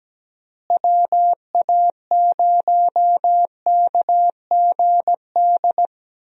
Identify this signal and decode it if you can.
Morse code, heard as a keyed tone, about 17 wpm, 700 Hz. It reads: WA0KGD
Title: USA Call Signs - 6 Characters - Repeated 2x - 17wpm